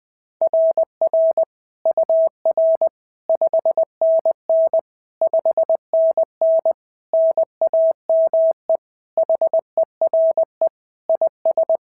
Morse code, heard as a keyed tone, about 20 wpm, 690 Hz. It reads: RR UR 5NN 5NN NAME HERE IS
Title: SOTA QSO Practice 20wpm